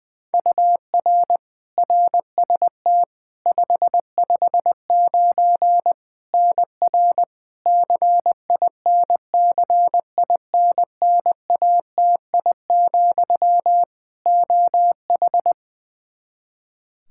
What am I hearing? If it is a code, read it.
Morse code, heard as a keyed tone, about 20 wpm, 700 Hz. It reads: UR RST 559 NR CINCINNATI, OH
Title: K1USN SST QSO Practice 20wpm